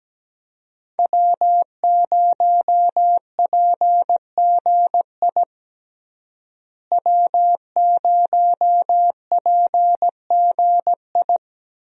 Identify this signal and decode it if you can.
Morse code, heard as a keyed tone, about 17 wpm, 700 Hz. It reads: W0PGI W0PGI
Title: USA Call Signs - 5 Characters - Repeated 3x - 17wpm